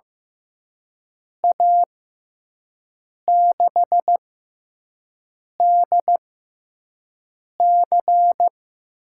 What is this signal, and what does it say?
Morse code, heard as a keyed tone, about 15 wpm, spaced as for 5 wpm, 700 Hz. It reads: A6DC
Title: USA Call Signs - 5 Characters - 6x Character Spacing - 15wpm